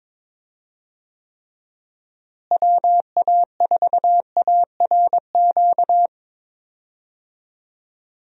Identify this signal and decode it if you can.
Morse code, heard as a keyed tone, about 21 wpm, 700 Hz. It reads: WA4ARQ
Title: USA Call Signs - 6 Characters - Repeated 3x - 22wpm